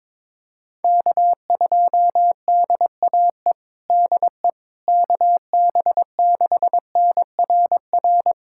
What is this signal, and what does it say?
Morse code, heard as a keyed tone, about 22 wpm, 700 Hz. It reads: K2DAE DE KB6NRR
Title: USA Callsign DE Callsign 22wpm